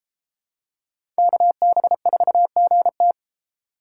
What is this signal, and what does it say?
Morse code, heard as a keyed tone, about 32 wpm, 700 Hz. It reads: KB4GT